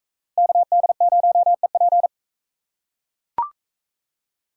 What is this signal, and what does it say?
Morse code, heard as a keyed tone, about 41 wpm, 700 Hz. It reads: KD0EP